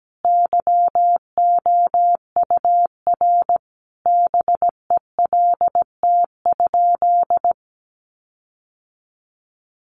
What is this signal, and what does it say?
Morse code, heard as a keyed tone, about 17 wpm, 700 Hz. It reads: YOUR BELT?
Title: Sentences from Top 1400 Words 17wpm